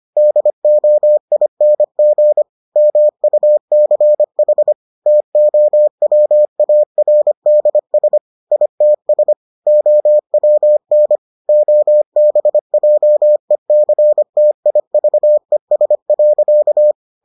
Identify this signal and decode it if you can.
Morse code, heard as a keyed tone, about 25 wpm, 600 Hz. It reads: DOING MUCH TOWARDS ITS OWN OBJECTIVES.